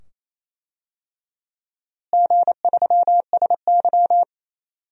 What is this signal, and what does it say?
Morse code, heard as a keyed tone, about 28 wpm, 700 Hz. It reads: G3SY
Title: DXCC-100 Call Signs - 4 Characters - Repeated 2x - 28wpm